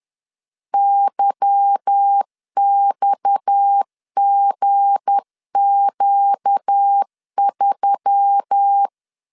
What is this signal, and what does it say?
Morse code, heard as a keyed tone, about 10 wpm, 800 Hz. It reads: YXGQ3
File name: LEZIONI / 8 YJ45 / 8 Y 10wpm 2615